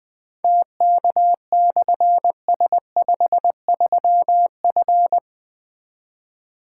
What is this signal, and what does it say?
Morse code, heard as a keyed tone, about 20 wpm, 700 Hz. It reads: TK/S53F
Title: Difficult Call Signs 20wpm